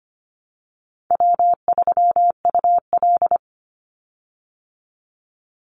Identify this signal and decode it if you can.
Morse code, heard as a keyed tone, about 25 wpm, 700 Hz. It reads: W3UL
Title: Easy Call Signs Repeated 3x - 25wpm